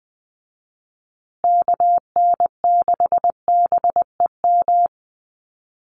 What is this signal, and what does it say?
Morse code, heard as a keyed tone, about 20 wpm, 700 Hz. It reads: KN6BEM
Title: USA Call Signs Repeated 3x 20wpm